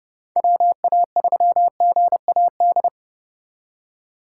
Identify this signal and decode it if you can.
Morse code, heard as a keyed tone, about 30 wpm, 700 Hz. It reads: WA3GAD